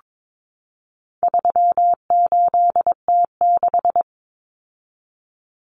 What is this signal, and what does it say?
Morse code, heard as a keyed tone, about 22 wpm, 700 Hz. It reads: 38T6
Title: Common Cut Numbers - 4-Digit 22wpm